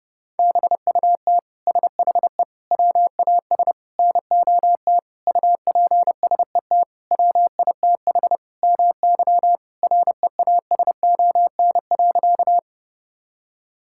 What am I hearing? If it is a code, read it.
Morse code, heard as a keyed tone, about 30 wpm, 700 Hz. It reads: BUT SHE WAS NOT UPSET WITH MY REASON.
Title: Sentences from Top 1000 Words - Review Part 5 - 30wpm